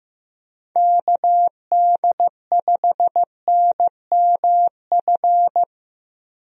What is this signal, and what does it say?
Morse code, heard as a keyed tone, about 15 wpm, 700 Hz. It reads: KD5NMF
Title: USA Call Signs - 6 Characters - Repeated 3x - 15wpm